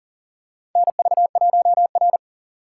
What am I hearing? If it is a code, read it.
Morse code, heard as a keyed tone, about 37 wpm, 700 Hz. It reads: NV1R
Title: USA Call Signs - 4x Character Spacing - 40wpm